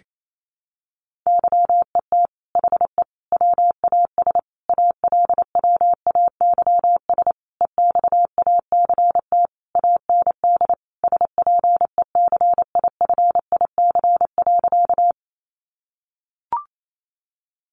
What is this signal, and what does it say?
Morse code, heard as a keyed tone, about 28 wpm, 700 Hz. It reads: YET HE WAS ALWAYS EXACT AND SPECIFIC.
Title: Sentences from Top 1300 Words 28wpm